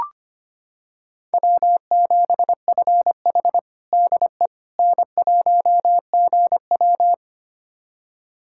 Morse code, heard as a keyed tone, about 25 wpm, 700 Hz. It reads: W7FH DE N1GW
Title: USA Callsign DE Callsign - 4 Characters - 25wpm